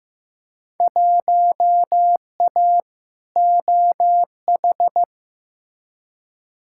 Morse code, heard as a keyed tone, about 15 wpm, 700 Hz. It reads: 1A OH
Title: Field Day Exchanges - 15wpm